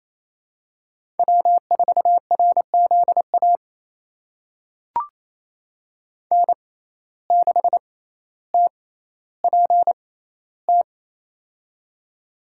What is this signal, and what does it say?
Morse code, heard as a keyed tone, about 27 wpm, spaced as for 14 wpm, 700 Hz. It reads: W4RZA N6TPT
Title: USA Call Signs - 5 Characters - 6x Character Spacing - 28wpm